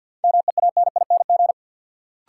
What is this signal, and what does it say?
Morse code, heard as a keyed tone, about 52 wpm, 700 Hz. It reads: MEANING
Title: Sentences from Top 900 Words - New Word Drill - 50wpm